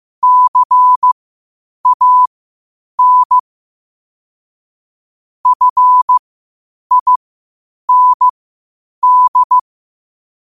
Morse code, heard as a keyed tone, about 15 wpm, spaced as for 8 wpm, 1000 Hz. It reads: CAN FIND
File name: qotdfiles / 8wpm